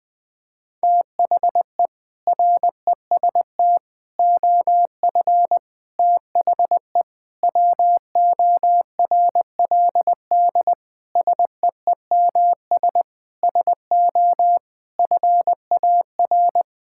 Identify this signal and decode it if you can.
Morse code, heard as a keyed tone, about 20 wpm, 700 Hz. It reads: THE REST OF THE WORLD SEEMS SO FAR
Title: Sentences with 5-Word Phrases - Spoken Beginning - 20wpm